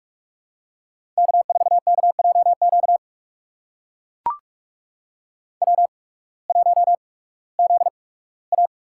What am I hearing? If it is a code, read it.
Morse code, heard as a keyed tone, about 44 wpm, spaced as for 21 wpm, 700 Hz. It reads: K4KJQ W1ZA
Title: USA Call Signs - 8x Character Spacing - 45wpm